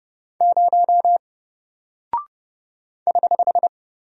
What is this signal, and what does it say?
Morse code, heard as a keyed tone, about 29 wpm, 700 Hz. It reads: 0 <HH>
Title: Single Character (Letter:Number:Punctuation:Prosign) 30wpm